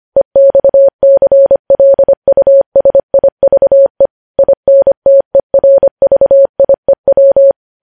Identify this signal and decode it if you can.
Morse code, heard as a keyed tone, about 25 wpm, 550 Hz. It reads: EXCLUSIVE INTERVIEW
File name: podcast / News headlines-1x-25x25